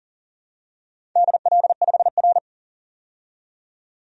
Morse code, heard as a keyed tone, about 40 wpm, 700 Hz. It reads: DL5R
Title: DXCC-100 Call Signs - 4 Characters - 40wpm